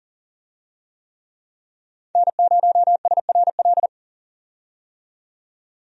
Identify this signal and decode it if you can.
Morse code, heard as a keyed tone, about 40 wpm, 700 Hz. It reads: N0SRL
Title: USA Call Signs - 5 Characters - 40wpm